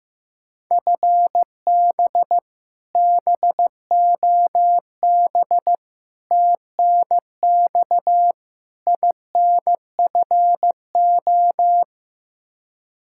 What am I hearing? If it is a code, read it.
Morse code, heard as a keyed tone, about 15 wpm, 700 Hz. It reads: FB BOB TNX INFO
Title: K1USN SST QSO Practice 15wpm